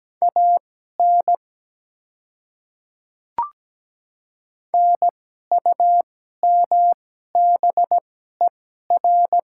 Morse code, heard as a keyed tone, about 17 wpm, spaced as for 12 wpm, 700 Hz. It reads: AN NUMBER
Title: 2x Character Spacing - Top 100 Words - 17wpm